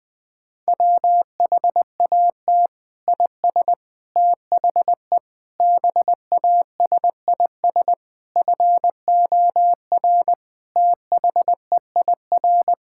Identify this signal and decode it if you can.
Morse code, heard as a keyed tone, about 19 wpm, 700 Hz. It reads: WHAT IS THE BASIS FOR THEIR
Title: Sentences from Top 1000 Words - Review Part 6 - 20wpm